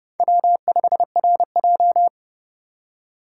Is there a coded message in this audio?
Yes: Morse code from 0.2 to 2.1 s, heard as a keyed tone, about 28 wpm, 700 Hz. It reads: W5RJ